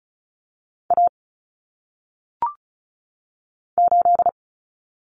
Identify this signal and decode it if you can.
Morse code, heard as a keyed tone, about 36 wpm, 700 Hz. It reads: A 8
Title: Common Cut Numbers - 1-Digit 35wpm